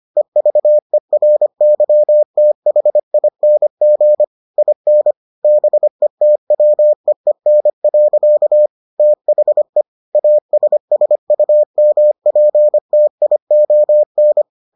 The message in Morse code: EVERYTHING IN BETWEEN. THE ASSUMPTION

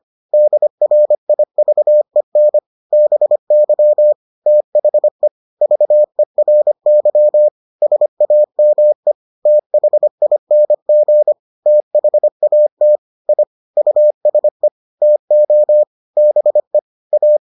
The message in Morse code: DRIVEN BY THE VERY SAME THING THAT I USE TO BE A

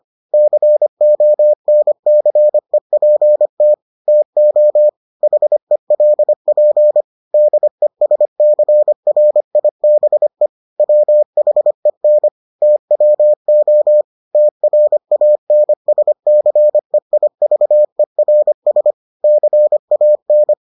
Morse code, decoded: CONCEPT TO HELP DESCRIBE WHEN TWO TRANSCEIVERS CAN

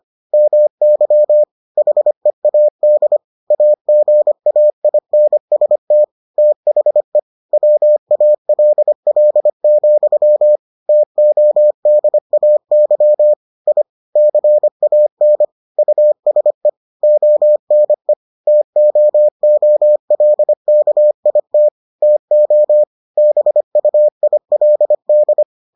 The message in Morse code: MY HEAD AGAINST THE WALL, TODAY I CAN USE ONE TOOLKIT TO BUILD

